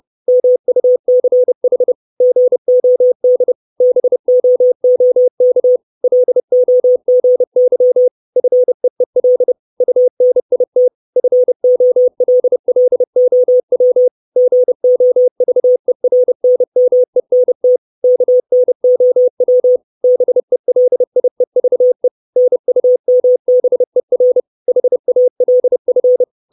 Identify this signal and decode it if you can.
Morse code, heard as a keyed tone, about 30 wpm, 500 Hz. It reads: MUCH GOD BOOK LOGY FEEL UNIT FOLLOW GOVERNMENT KNOW BELIEVE NUMBER HALF
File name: 632 Words 30WPM